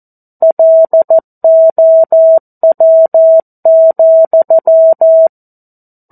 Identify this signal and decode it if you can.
Morse code, heard as a keyed tone, about 14 wpm, 650 Hz. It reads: LOW,
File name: 02-Chapter-02 14 WPM